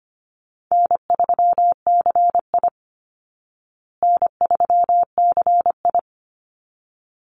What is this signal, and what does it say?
Morse code, heard as a keyed tone, about 25 wpm, 700 Hz. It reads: N3CI N3CI
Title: Call Signs Repeated 3x - 25wpm